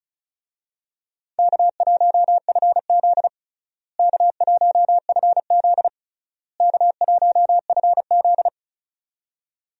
Morse code, heard as a keyed tone, about 34 wpm, 700 Hz. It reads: K1FZ K1FZ K1FZ
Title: USA Call Signs - 4 Characters - Repeated 3x - 35wpm